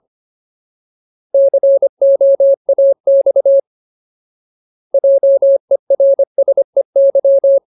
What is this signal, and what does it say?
Morse code, heard as a keyed tone, about 25 wpm, 550 Hz. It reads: COAX JERSEY
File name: Ultimate Word List / 25WPM Ultimate